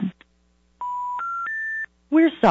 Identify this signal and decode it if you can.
Special information tone, with speech.
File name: 575-385-9999 valor